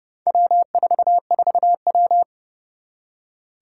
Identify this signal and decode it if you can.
Morse code, heard as a keyed tone, about 30 wpm, 700 Hz. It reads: W44W
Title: Four Challenging Letter-Number Combinations 30wpm